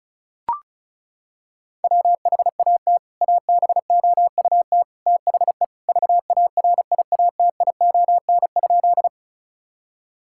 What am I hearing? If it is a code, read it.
Morse code, heard as a keyed tone, about 36 wpm, 700 Hz. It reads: WHAT ABOUT THE VARIATION?
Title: Sentences from Top 1600 Words 35wpm